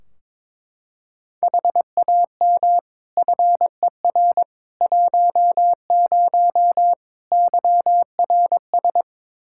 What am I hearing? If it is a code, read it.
Morse code, heard as a keyed tone, about 22 wpm, 700 Hz. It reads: HAM FER 10 YRS
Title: Practice QSO - Exchange 3 - More Info - 22wpm